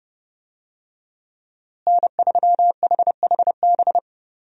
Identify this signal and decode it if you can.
Morse code, heard as a keyed tone, about 31 wpm, 700 Hz. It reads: N3HHB
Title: USA Call Signs - 5 Characters - Repeated 2x - 30wpm